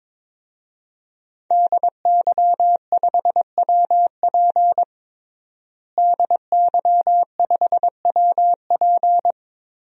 Morse code, heard as a keyed tone, about 22 wpm, 700 Hz. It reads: DY5WP DY5WP
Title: DXCC-100 Call Signs - 5 Characters - Repeated 3x - 22wpm